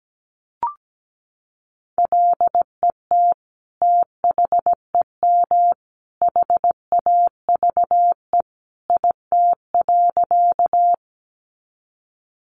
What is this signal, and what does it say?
Morse code, heard as a keyed tone, about 17 wpm, 700 Hz. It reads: LET THEM HAVE IT.
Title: Sentences from Top 300 Words 17wpm